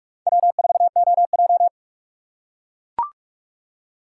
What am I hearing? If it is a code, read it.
Morse code, heard as a keyed tone, about 44 wpm, 700 Hz. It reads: W4OJ